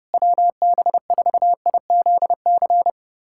Morse code, heard as a keyed tone, about 30 wpm, 700 Hz. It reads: WB4IZC
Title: USA Callsign DE Callsign - 6 Characters - 30wpm